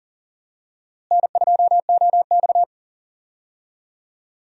Morse code, heard as a keyed tone, about 40 wpm, 700 Hz. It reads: N2OX
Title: USA Call Signs - 4 Characters - Repeated 3x - 40wpm